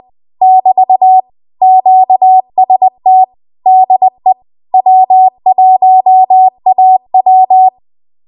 Morse code, heard as a keyed tone, about 20 wpm, 750 Hz. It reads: <BT> QST DE W1AW
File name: Archive / 20 WPM / 20WPM